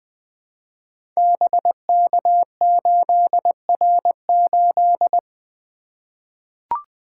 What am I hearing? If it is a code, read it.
Morse code, heard as a keyed tone, about 20 wpm, 700 Hz. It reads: BK8R8
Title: Code Groups - 5 Letter:Numbers - 20wpm